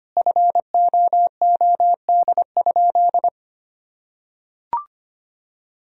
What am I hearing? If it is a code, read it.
Morse code, heard as a keyed tone, about 25 wpm, 700 Hz. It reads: FOOD?
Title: Sentences from Top 1000 Words - Review Part 3 - 25wpm